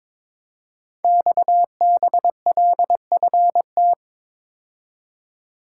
Morse code, heard as a keyed tone, about 23 wpm, 700 Hz. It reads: XBLFT
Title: Code Groups - 5 Letters - 22wpm